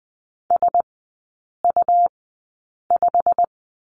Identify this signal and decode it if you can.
Morse code, heard as a keyed tone, about 20 wpm, spaced as for 8 wpm, 700 Hz. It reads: SU5
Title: 107 Lesson 18e - Letter m - Three Characters 20wpm